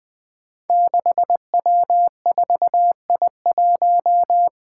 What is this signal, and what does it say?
Morse code, heard as a keyed tone, about 20 wpm, 700 Hz. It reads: 6W4I1
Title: Five Challenging Letter-Number Combinations 20wpm